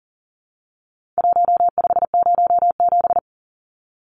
Morse code, heard as a keyed tone, about 40 wpm, 700 Hz. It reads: 1507
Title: Numbers - Contest Style - 4-Digit 40wpm